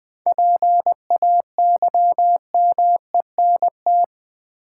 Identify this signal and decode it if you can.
Morse code, heard as a keyed tone, about 21 wpm, 700 Hz. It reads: PAYMENT
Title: 7 Letter Words - First 3 Letters Given - 20wpm